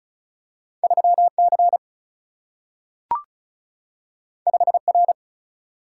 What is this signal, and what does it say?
Morse code, heard as a keyed tone, about 36 wpm, 700 Hz. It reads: 3C 5R
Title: Single-Letter Single-Number 35wpm